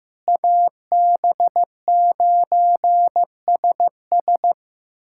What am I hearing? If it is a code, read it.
Morse code, heard as a keyed tone, about 15 wpm, 700 Hz. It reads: AB9SS
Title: USA Callsign DE Callsign - 5 Characters - 15wpm